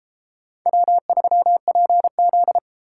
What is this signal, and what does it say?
Morse code, heard as a keyed tone, about 29 wpm, 700 Hz. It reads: W3PZ